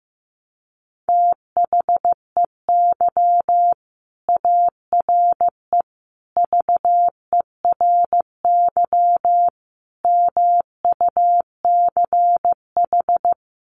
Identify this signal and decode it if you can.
Morse code, heard as a keyed tone, about 15 wpm, 700 Hz. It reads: THEY ARE VERY MUCH
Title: Sentences from Top 1500 Words 15wpm